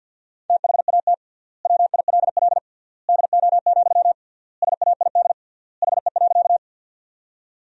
Morse code, heard as a keyed tone, about 50 wpm, 700 Hz. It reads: THAT WILL DO, SAID HE.
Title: 3x Word-Spacing - Sentences from Top 100 Words - 50wpm